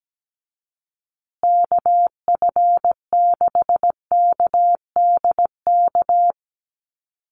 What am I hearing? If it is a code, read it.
Morse code, heard as a keyed tone, about 17 wpm, 700 Hz. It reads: KF6KDK
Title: USA Call Signs Repeated 3x 17wpm